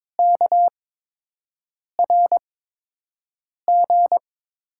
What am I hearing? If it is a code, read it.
Morse code, heard as a keyed tone, about 22 wpm, spaced as for 6 wpm, 700 Hz. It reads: KRG
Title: USA Call Signs - 8x Character Spacing - 22wpm